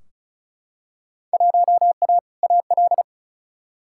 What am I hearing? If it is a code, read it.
Morse code, heard as a keyed tone, about 36 wpm, 700 Hz. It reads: 1A AL
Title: Field Day Exchanges - 35wpm